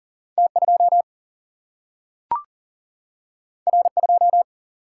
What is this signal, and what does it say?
Morse code, heard as a keyed tone, about 40 wpm, 700 Hz. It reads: T 2 R 2
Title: Single-Letter Single-Number 40wpm